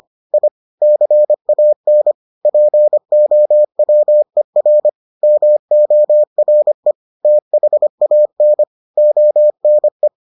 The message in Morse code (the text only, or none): I CAN POWER MORE THAN ONE